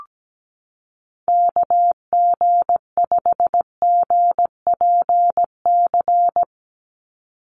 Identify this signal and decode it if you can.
Morse code, heard as a keyed tone, about 17 wpm, 700 Hz. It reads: KG5GPC
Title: USA Call Signs 17wpm